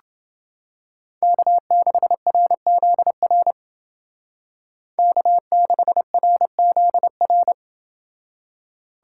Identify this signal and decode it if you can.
Morse code, heard as a keyed tone, about 28 wpm, 700 Hz. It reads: K6RZR K6RZR